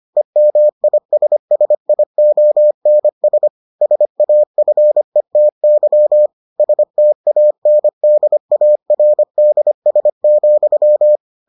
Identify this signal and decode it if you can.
Morse code, heard as a keyed tone, about 25 wpm, 600 Hz. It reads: EMISSIONS SAFETY STANDARDS,